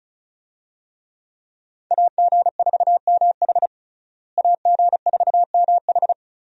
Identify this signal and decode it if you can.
Morse code, heard as a keyed tone, about 34 wpm, 700 Hz. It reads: AG4MH AG4MH
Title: USA Call Signs - 5 Characters - Repeated 3x - 35wpm